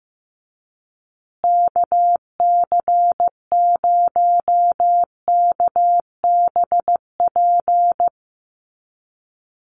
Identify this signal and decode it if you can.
Morse code, heard as a keyed tone, about 15 wpm, 700 Hz. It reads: KC0KBP
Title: USA Call Signs Repeated 3x 15wpm